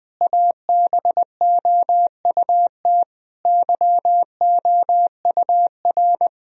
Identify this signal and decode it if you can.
Morse code, heard as a keyed tone, about 20 wpm, 700 Hz. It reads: ABOUT YOUR
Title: Sentences from Top 1000 Words - Review Part 3 - 20wpm